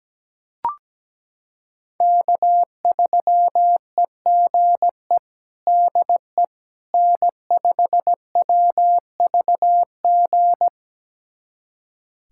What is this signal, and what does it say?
Morse code, heard as a keyed tone, about 17 wpm, 700 Hz. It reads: K3EGE DE N5WVG
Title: USA Callsign DE Callsign - 5 Characters - 17wpm